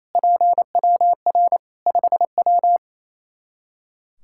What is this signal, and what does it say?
Morse code, heard as a keyed tone, about 27 wpm, 700 Hz. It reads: PWR 5W
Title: Practice QSO - Exchange 2 - Info - 28wpm